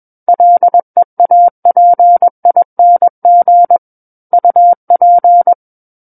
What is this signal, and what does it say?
Morse code, heard as a keyed tone, about 21 wpm, 700 Hz. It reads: LEAPING UP
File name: Book One / 12-Chapter 12 - 21 WPM